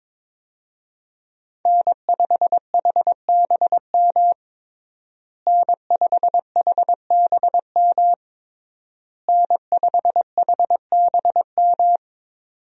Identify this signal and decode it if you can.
Morse code, heard as a keyed tone, about 22 wpm, 700 Hz. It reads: N5HBM N5HBM N5HBM
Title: USA Call Signs - 5 Characters - Repeated 3x - 22wpm